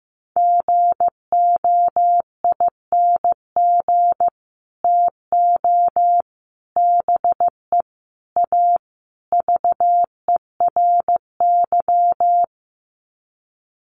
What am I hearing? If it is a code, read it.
Morse code, heard as a keyed tone, about 15 wpm, 700 Hz. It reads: GOING TO BE A VERY
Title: Sets of 5 Words Encore 15wpm